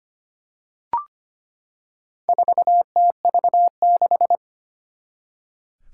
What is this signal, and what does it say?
Morse code, heard as a keyed tone, about 25 wpm, 700 Hz. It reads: 4TV6
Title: Four Challenging Letter-Number Combinations 25wpm